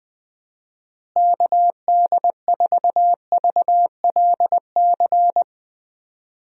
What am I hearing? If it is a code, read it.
Morse code, heard as a keyed tone, about 20 wpm, 700 Hz. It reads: KD4VLC
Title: USA Call Signs - 6 Characters - Repeated 2x - 20wpm